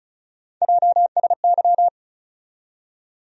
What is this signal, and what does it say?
Morse code, heard as a keyed tone, about 36 wpm, 700 Hz. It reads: JSY
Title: USA Call Sign Postfixes - 3 Letters 35wpm